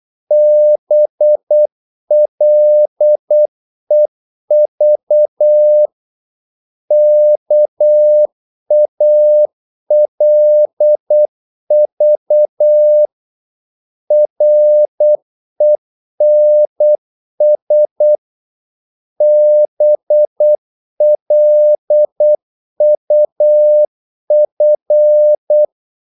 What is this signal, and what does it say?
Morse code, heard as a keyed tone, about 8 wpm, 600 Hz. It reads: BLEV KALV RENS BLUF
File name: Korte ord DK 8wpm